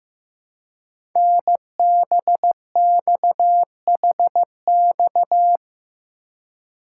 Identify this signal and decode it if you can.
Morse code, heard as a keyed tone, about 15 wpm, 700 Hz. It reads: NBXHX
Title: Code Groups - 5 Letters - 15wpm